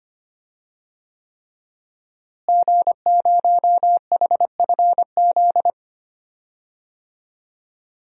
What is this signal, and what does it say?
Morse code, heard as a keyed tone, about 25 wpm, 700 Hz. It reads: G0HFZ
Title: DXCC-100 Call Signs - 5 Characters - Repeated 3x - 25wpm